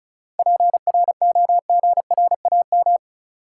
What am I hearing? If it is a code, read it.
Morse code, heard as a keyed tone, about 34 wpm, 700 Hz. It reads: PROGRAM
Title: Sentences from Top 1500 Words - New Word Drill Alt - 35wpm